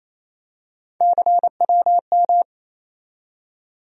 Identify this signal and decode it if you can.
Morse code, heard as a keyed tone, about 28 wpm, 700 Hz. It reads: CWM
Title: USA Call Sign Postfixes - 3 Letters 28wpm